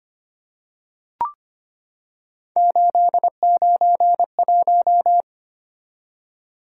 Morse code, heard as a keyed tone, about 25 wpm, 700 Hz. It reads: 891
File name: Three Letter-Number Combinations - 25wpm